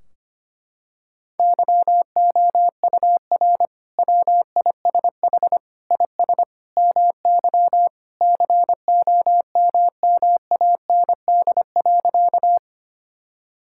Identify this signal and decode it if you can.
Morse code, heard as a keyed tone, about 25 wpm, 700 Hz. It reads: YOUR WISH IS MY COMMAND.
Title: Sentences from Top 1600 Words 25wpm